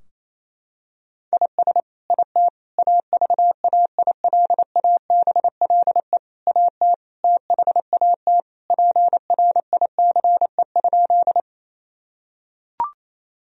Morse code, heard as a keyed tone, about 28 wpm, 700 Hz. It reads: IS IT AVAILABLE AT THAT PRICE?
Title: Sentences from Top 1000 Words - Review Part 3 - 28wpm